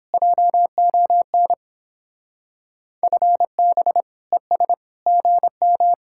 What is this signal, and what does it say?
Morse code, heard as a keyed tone, about 29 wpm, 700 Hz. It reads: JON FB ES GM